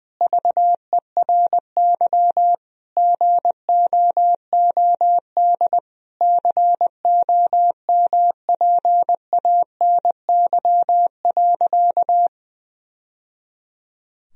Morse code, heard as a keyed tone, about 20 wpm, 700 Hz. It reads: VERY GOOD COMPANY.
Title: Sentences from Top 1000 Words - Review Part 3 - 20wpm